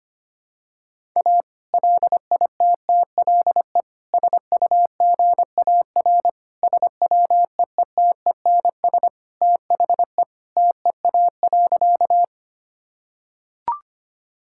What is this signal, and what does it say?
Morse code, heard as a keyed tone, about 25 wpm, 700 Hz. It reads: A LITTLE SUGAR SWEETENS THE TEA.
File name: practice / 25wpm / Sentences with 2-Word Phrases - 25wpm